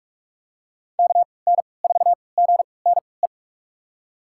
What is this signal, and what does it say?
Morse code, heard as a keyed tone, about 46 wpm, spaced as for 26 wpm, 700 Hz. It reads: KN4GNE
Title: USA Call Signs - 3x Character Spacing - 45wpm